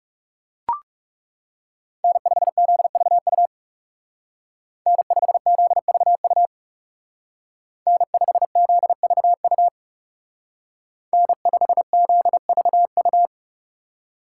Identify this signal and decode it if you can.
Morse code, heard as a keyed tone, about 36 wpm, 700 Hz. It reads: N5ZVU N5ZVU N5ZVU N5ZVU
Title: Speed Racing - USA 1x3 Call Signs 30wpm